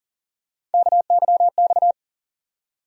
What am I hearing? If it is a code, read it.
Morse code, heard as a keyed tone, about 40 wpm, 700 Hz. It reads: KYX
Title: USA Call Sign Postfixes - 3 Letters 40wpm